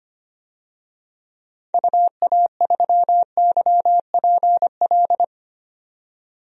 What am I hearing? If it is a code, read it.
Morse code, heard as a keyed tone, about 25 wpm, 700 Hz. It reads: UA3YPL
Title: DXCC-100 Call Signs - 6 Characters - 25wpm